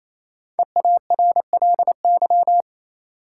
Morse code, heard as a keyed tone, about 28 wpm, 700 Hz. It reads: EARLY
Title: Sets of 3 Words - New Word Drill - 28wpm